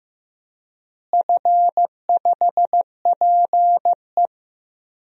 Morse code, heard as a keyed tone, about 15 wpm, 700 Hz. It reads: F5PE
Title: DXCC-100 Call Signs - 4 Characters - Repeated 2x - 15wpm